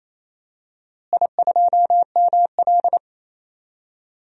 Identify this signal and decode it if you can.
Morse code, heard as a keyed tone, about 28 wpm, 700 Hz. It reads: I2ML
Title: DXCC-100 Call Signs - 4 Characters - Repeated 2x - 28wpm